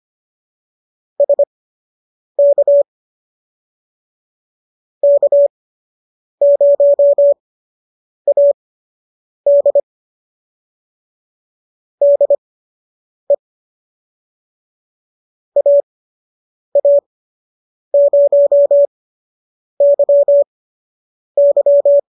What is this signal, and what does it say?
Morse code, heard as a keyed tone, about 25 wpm, spaced as for 8 wpm, 570 Hz. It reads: SK K0AD DE AA0YY